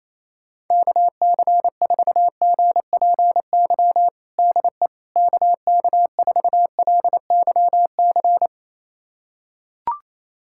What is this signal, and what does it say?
Morse code, heard as a keyed tone, about 28 wpm, 700 Hz. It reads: KC4GPY DE KK4LYC
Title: USA Callsign DE Callsign - 6 Characters - 28wpm